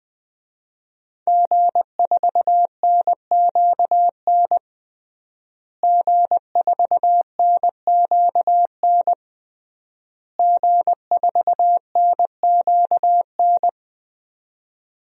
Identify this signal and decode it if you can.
Morse code, heard as a keyed tone, about 20 wpm, 700 Hz. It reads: G4NQN G4NQN G4NQN
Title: DXCC-100 Call Signs - 5 Characters - Repeated 3x - 20wpm